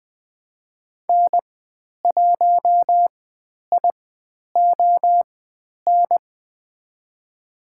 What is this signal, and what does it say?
Morse code, heard as a keyed tone, about 20 wpm, spaced as for 10 wpm, 700 Hz. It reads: N1ION
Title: 095 Lesson 16g - Letter c - Call Signs 20wpm